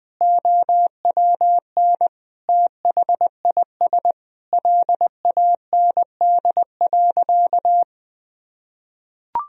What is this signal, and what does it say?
Morse code, heard as a keyed tone, about 20 wpm, 700 Hz. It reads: OWN THIS LAND.
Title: Sentences from Top 1000 Words - Review Part 4 - 20wpm